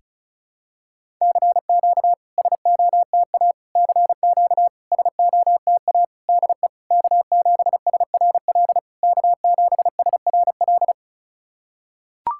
Morse code, heard as a keyed tone, about 36 wpm, 690 Hz. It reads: CQ SOTA CQ SOTA DE K7SRL K7SRL
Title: SOTA QSO Practice 35wpm